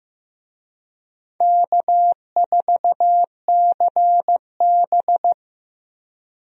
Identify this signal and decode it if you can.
Morse code, heard as a keyed tone, about 15 wpm, 700 Hz. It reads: K4CB
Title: USA Call Signs - 4 Characters - 15wpm